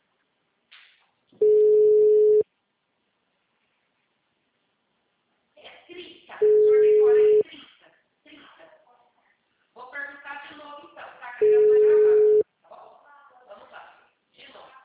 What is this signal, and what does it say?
Ringback tone (Europe, 425 Hz), with speech.
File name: out- -2013- - - .88481